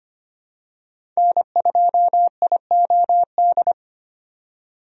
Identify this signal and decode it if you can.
Morse code, heard as a keyed tone, about 25 wpm, 690 Hz. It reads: N2IOD
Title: USA Call Signs - 5 Characters - 25wpm